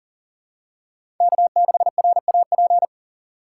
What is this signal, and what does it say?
Morse code, heard as a keyed tone, about 40 wpm, 700 Hz. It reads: K6RAP
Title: USA Call Signs - 5 Characters - 6x Character Spacing - 40wpm